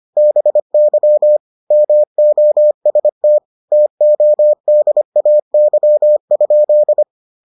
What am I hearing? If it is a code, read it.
Morse code, heard as a keyed tone, about 25 wpm, 600 Hz. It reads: BY MOST TODAY?